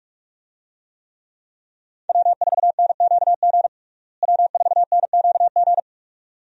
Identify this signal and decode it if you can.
Morse code, heard as a keyed tone, about 42 wpm, 700 Hz. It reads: W4NQG W4NQG
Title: USA Call Signs - 5 Characters - Repeated 2x - 45wpm